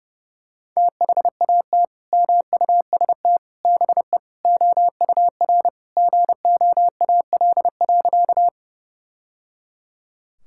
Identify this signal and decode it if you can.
Morse code, heard as a keyed tone, about 30 wpm, 700 Hz. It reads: THAT MUST BE OUR GOAL.
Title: Sentences from Top 1000 Words - Review Part 5 - 30wpm